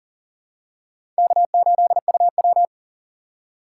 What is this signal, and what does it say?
Morse code, heard as a keyed tone, about 40 wpm, 700 Hz. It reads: K8UW
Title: USA Call Signs - 4 Characters - 6x Character Spacing - 40wpm